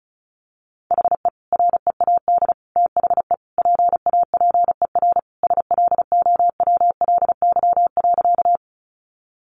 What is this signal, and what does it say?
Morse code, heard as a keyed tone, about 34 wpm, 700 Hz. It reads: HE READ THE PAPER SLOWLY.
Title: Sentences from Top 1100 Words 35wpm